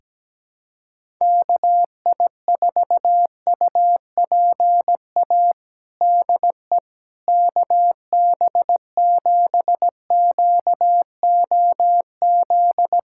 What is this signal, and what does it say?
Morse code, heard as a keyed tone, about 17 wpm, 700 Hz. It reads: KI4UPA DE KB7QOZ
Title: USA Callsign DE Callsign 17wpm